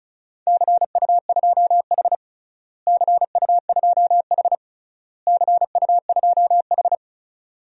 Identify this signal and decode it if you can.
Morse code, heard as a keyed tone, about 34 wpm, 700 Hz. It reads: CU2H CU2H CU2H
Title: DXCC-100 Call Signs - 4 Characters - Repeated 3x - 35wpm